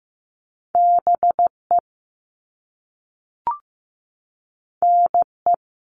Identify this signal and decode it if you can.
Morse code, heard as a keyed tone, about 15 wpm, 700 Hz. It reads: BE NE
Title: All Cut Numbers - 2-Digit 15wpm